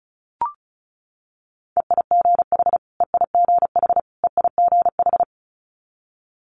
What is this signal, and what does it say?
Morse code, heard as a keyed tone, about 36 wpm, 700 Hz. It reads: EIGH EIGH EIGH
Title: Multi-letter Phonemes Repeated 3x 35wpm